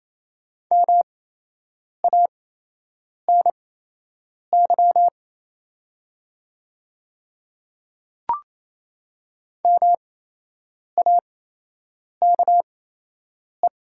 Morse code, heard as a keyed tone, about 28 wpm, spaced as for 8 wpm, 700 Hz. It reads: MANY MAKE
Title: 8x Character Spacing - Top 300 Words - 28wpm